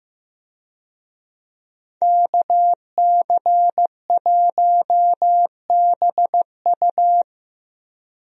Morse code, heard as a keyed tone, about 16 wpm, 700 Hz. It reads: KC1BU
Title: USA Call Signs - 5 Characters - Repeated 3x - 15wpm